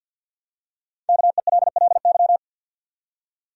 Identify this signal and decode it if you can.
Morse code, heard as a keyed tone, about 51 wpm, 700 Hz. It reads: KELLY
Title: USA Common Names - Max 5 Letters - 50wpm